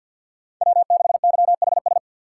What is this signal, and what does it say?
Morse code, heard as a keyed tone, about 49 wpm, 700 Hz. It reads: W6YHS
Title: USA Call Signs - 5 Characters - 4x Character Spacing - 50wpm